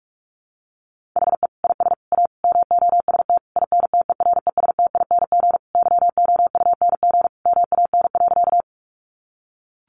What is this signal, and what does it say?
Morse code, heard as a keyed tone, about 44 wpm, 700 Hz. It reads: HE IS A MOST INTERESTING YOUNG MAN.
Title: Sentences from Top 1000 Words 45wpm